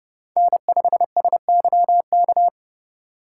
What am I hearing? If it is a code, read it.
Morse code, heard as a keyed tone, about 35 wpm, 700 Hz. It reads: N5SYK
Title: Speed Racing - USA 1x3 Call Signs 30wpm